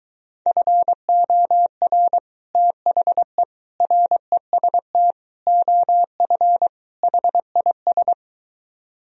Morse code, heard as a keyed tone, about 23 wpm, 700 Hz. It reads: FOR THE REST OF HIS